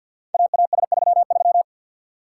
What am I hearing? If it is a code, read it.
Morse code, heard as a keyed tone, about 47 wpm, 700 Hz. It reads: AAS33
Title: Code Groups - 5 Letter:Numbers - 50wpm